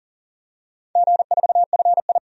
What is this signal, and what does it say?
Morse code, heard as a keyed tone, about 37 wpm, 700 Hz. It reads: G4FI
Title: DXCC-100 Call Signs - 4 Characters - 40wpm